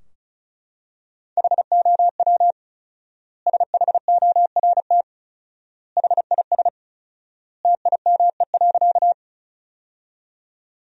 Morse code, heard as a keyed tone, about 34 wpm, 700 Hz. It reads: HOW SHORT HIS TIME.
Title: 4x Word-Spacing - Sentences from Top 400 Words - 35wpm